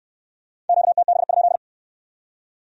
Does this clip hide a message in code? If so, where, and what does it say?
Yes: Morse code from 0.7 to 1.6 s, heard as a keyed tone, about 66 wpm, 700 Hz. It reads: QTB?